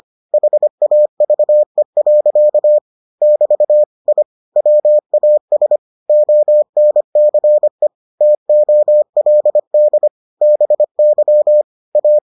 Morse code, heard as a keyed tone, about 25 wpm, 600 Hz. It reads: HAVE. <BT> I WAS ONCE TOLD BY A